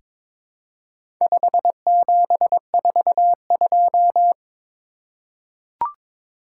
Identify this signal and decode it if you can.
Morse code, heard as a keyed tone, about 21 wpm, 700 Hz. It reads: 5742
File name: practice / 22wpm / Four Letter-Number Combinations - 22wpm